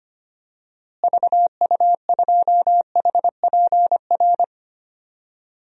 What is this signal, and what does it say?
Morse code, heard as a keyed tone, about 25 wpm, 700 Hz. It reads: VU2HPR
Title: DXCC-100 Call Signs - 6 Characters - Repeated 2x - 25wpm